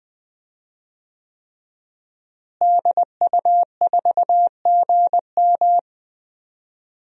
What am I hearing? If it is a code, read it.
Morse code, heard as a keyed tone, about 20 wpm, 700 Hz. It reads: DU4GM
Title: DXCC-100 Call Signs - 5 Characters - Repeated 3x - 20wpm